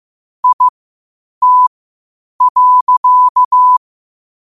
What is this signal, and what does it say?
Morse code, heard as a keyed tone, about 15 wpm, spaced as for 8 wpm, 1000 Hz. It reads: IT.
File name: qotdfiles / 8wpm